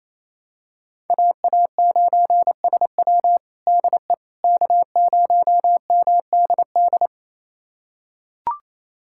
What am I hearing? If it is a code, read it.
Morse code, heard as a keyed tone, about 28 wpm, 700 Hz. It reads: AA9SW DE K0MDD
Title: USA Callsign DE Callsign 28wpm